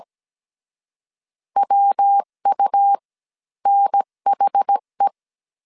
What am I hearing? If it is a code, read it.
Morse code, heard as a keyed tone, about 17 wpm, spaced as for 13 wpm, 800 Hz. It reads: WUNHE
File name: CW / DET / 1 6 1500C S C 16wpm 2733